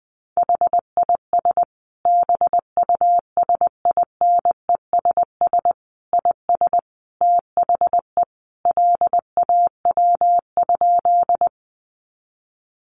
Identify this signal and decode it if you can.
Morse code, heard as a keyed tone, about 20 wpm, 700 Hz. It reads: HIS BUSINESS IS THE LAW?
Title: Sentences from Top 600 Words 20wpm